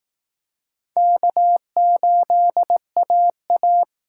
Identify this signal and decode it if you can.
Morse code, heard as a keyed tone, about 19 wpm, 700 Hz. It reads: K8AA